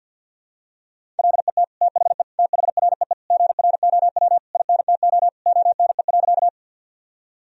Morse code, heard as a keyed tone, about 51 wpm, 700 Hz. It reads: LET THE THREE GROW INTO ONE.
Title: Sentences from Top 1000 Words - Review Part 1 - 50wpm